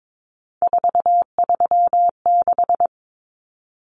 Morse code, heard as a keyed tone, about 22 wpm, 700 Hz. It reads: 436
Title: Common Cut Numbers - 3-Digit 22wpm